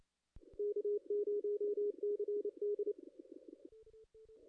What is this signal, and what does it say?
Morse code, heard as a keyed tone, about 28 wpm, 440 Hz. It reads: K0CD
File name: contests / ss-cw-k5tr-op-n5rz -2300